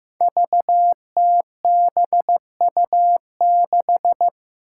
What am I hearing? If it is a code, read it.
Morse code, heard as a keyed tone, about 15 wpm, 700 Hz. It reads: VTBU6
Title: Five Challenging Letter-Number Combinations 15wpm